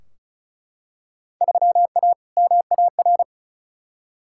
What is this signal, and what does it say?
Morse code, heard as a keyed tone, about 36 wpm, 700 Hz. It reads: 3A MAR
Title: Field Day Exchanges Encore - 35wpm